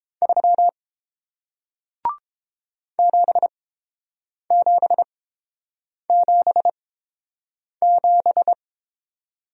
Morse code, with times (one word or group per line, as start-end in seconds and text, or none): 0.2-0.7 3
3.0-8.5 7777